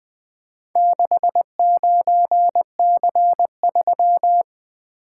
Morse code, heard as a keyed tone, about 22 wpm, 700 Hz. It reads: 69C3
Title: Speed-Racing - Four Letter-Number Combinations - 20wpm